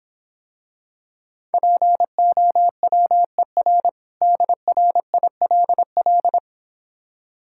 Morse code, heard as a keyed tone, about 26 wpm, 700 Hz. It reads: POWER DRILL